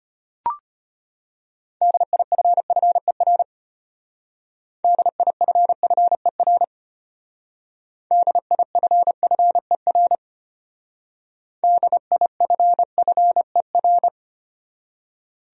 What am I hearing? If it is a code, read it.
Morse code, heard as a keyed tone, about 30 wpm, 700 Hz. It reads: DIFFER DIFFER DIFFER DIFFER
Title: Speed-Racing Top 500 Words 25wpm